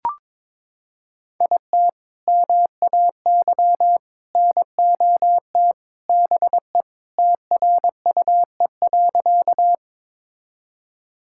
Morse code, heard as a keyed tone, about 22 wpm, 700 Hz. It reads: IT MAY NOT BE TRUE.
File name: practice / 22wpm / Sentences with 3-Word Phrases - 22wpm